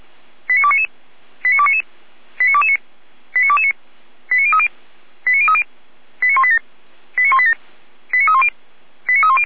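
Five-tone selective call (ZVEI), heard as quick runs of five tones, tones of about 70 ms. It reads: 89299 89299 89229 89229 89030 89030 89187 89187 89310 89310
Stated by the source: Five-tone ZVEI selective call 89240, received by SDR, feuerwehr-orlamuende fire brigade.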